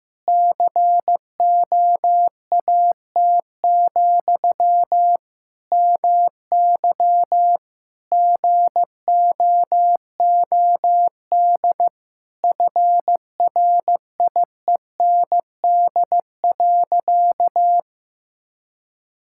Morse code, heard as a keyed tone, about 15 wpm, 700 Hz. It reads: COAT, MY GOOD FRIEND.
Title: Sentences from Top 1600 Words 15wpm